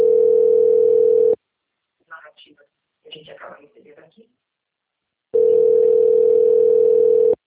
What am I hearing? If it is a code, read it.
Ringback tone (North America), 440 + 480 Hz, with speech.
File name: out- - - - - .23865